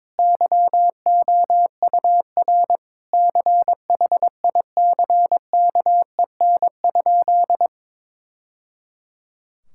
Morse code, heard as a keyed tone, about 21 wpm, 700 Hz. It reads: YOUR CHICKEN?
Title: Sentences from Top 1000 Words - Review Part 5 - 22wpm